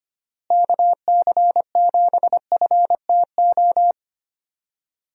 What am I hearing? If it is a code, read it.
Morse code, heard as a keyed tone, about 25 wpm, 700 Hz. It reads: KC7FTO
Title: USA Call Signs - 6 Characters - Repeated 2x - 25wpm